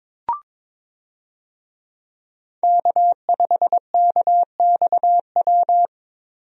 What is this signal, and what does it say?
Morse code, heard as a keyed tone, about 21 wpm, 700 Hz. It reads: K5KXW
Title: USA Call Signs - 5 Characters - Repeated 3x - 22wpm